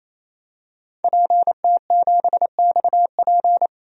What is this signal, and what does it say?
Morse code, heard as a keyed tone, about 28 wpm, 700 Hz. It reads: PT7XP
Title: DXCC-100 Call Signs - 5 Characters - 28wpm